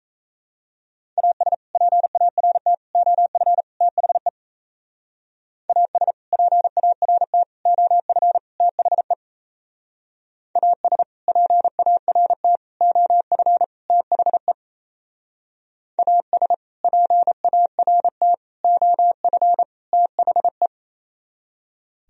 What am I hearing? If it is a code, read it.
Morse code, heard as a keyed tone, about 36 wpm, 700 Hz. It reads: AS PART OF THE AS PART OF THE AS PART OF THE AS PART OF THE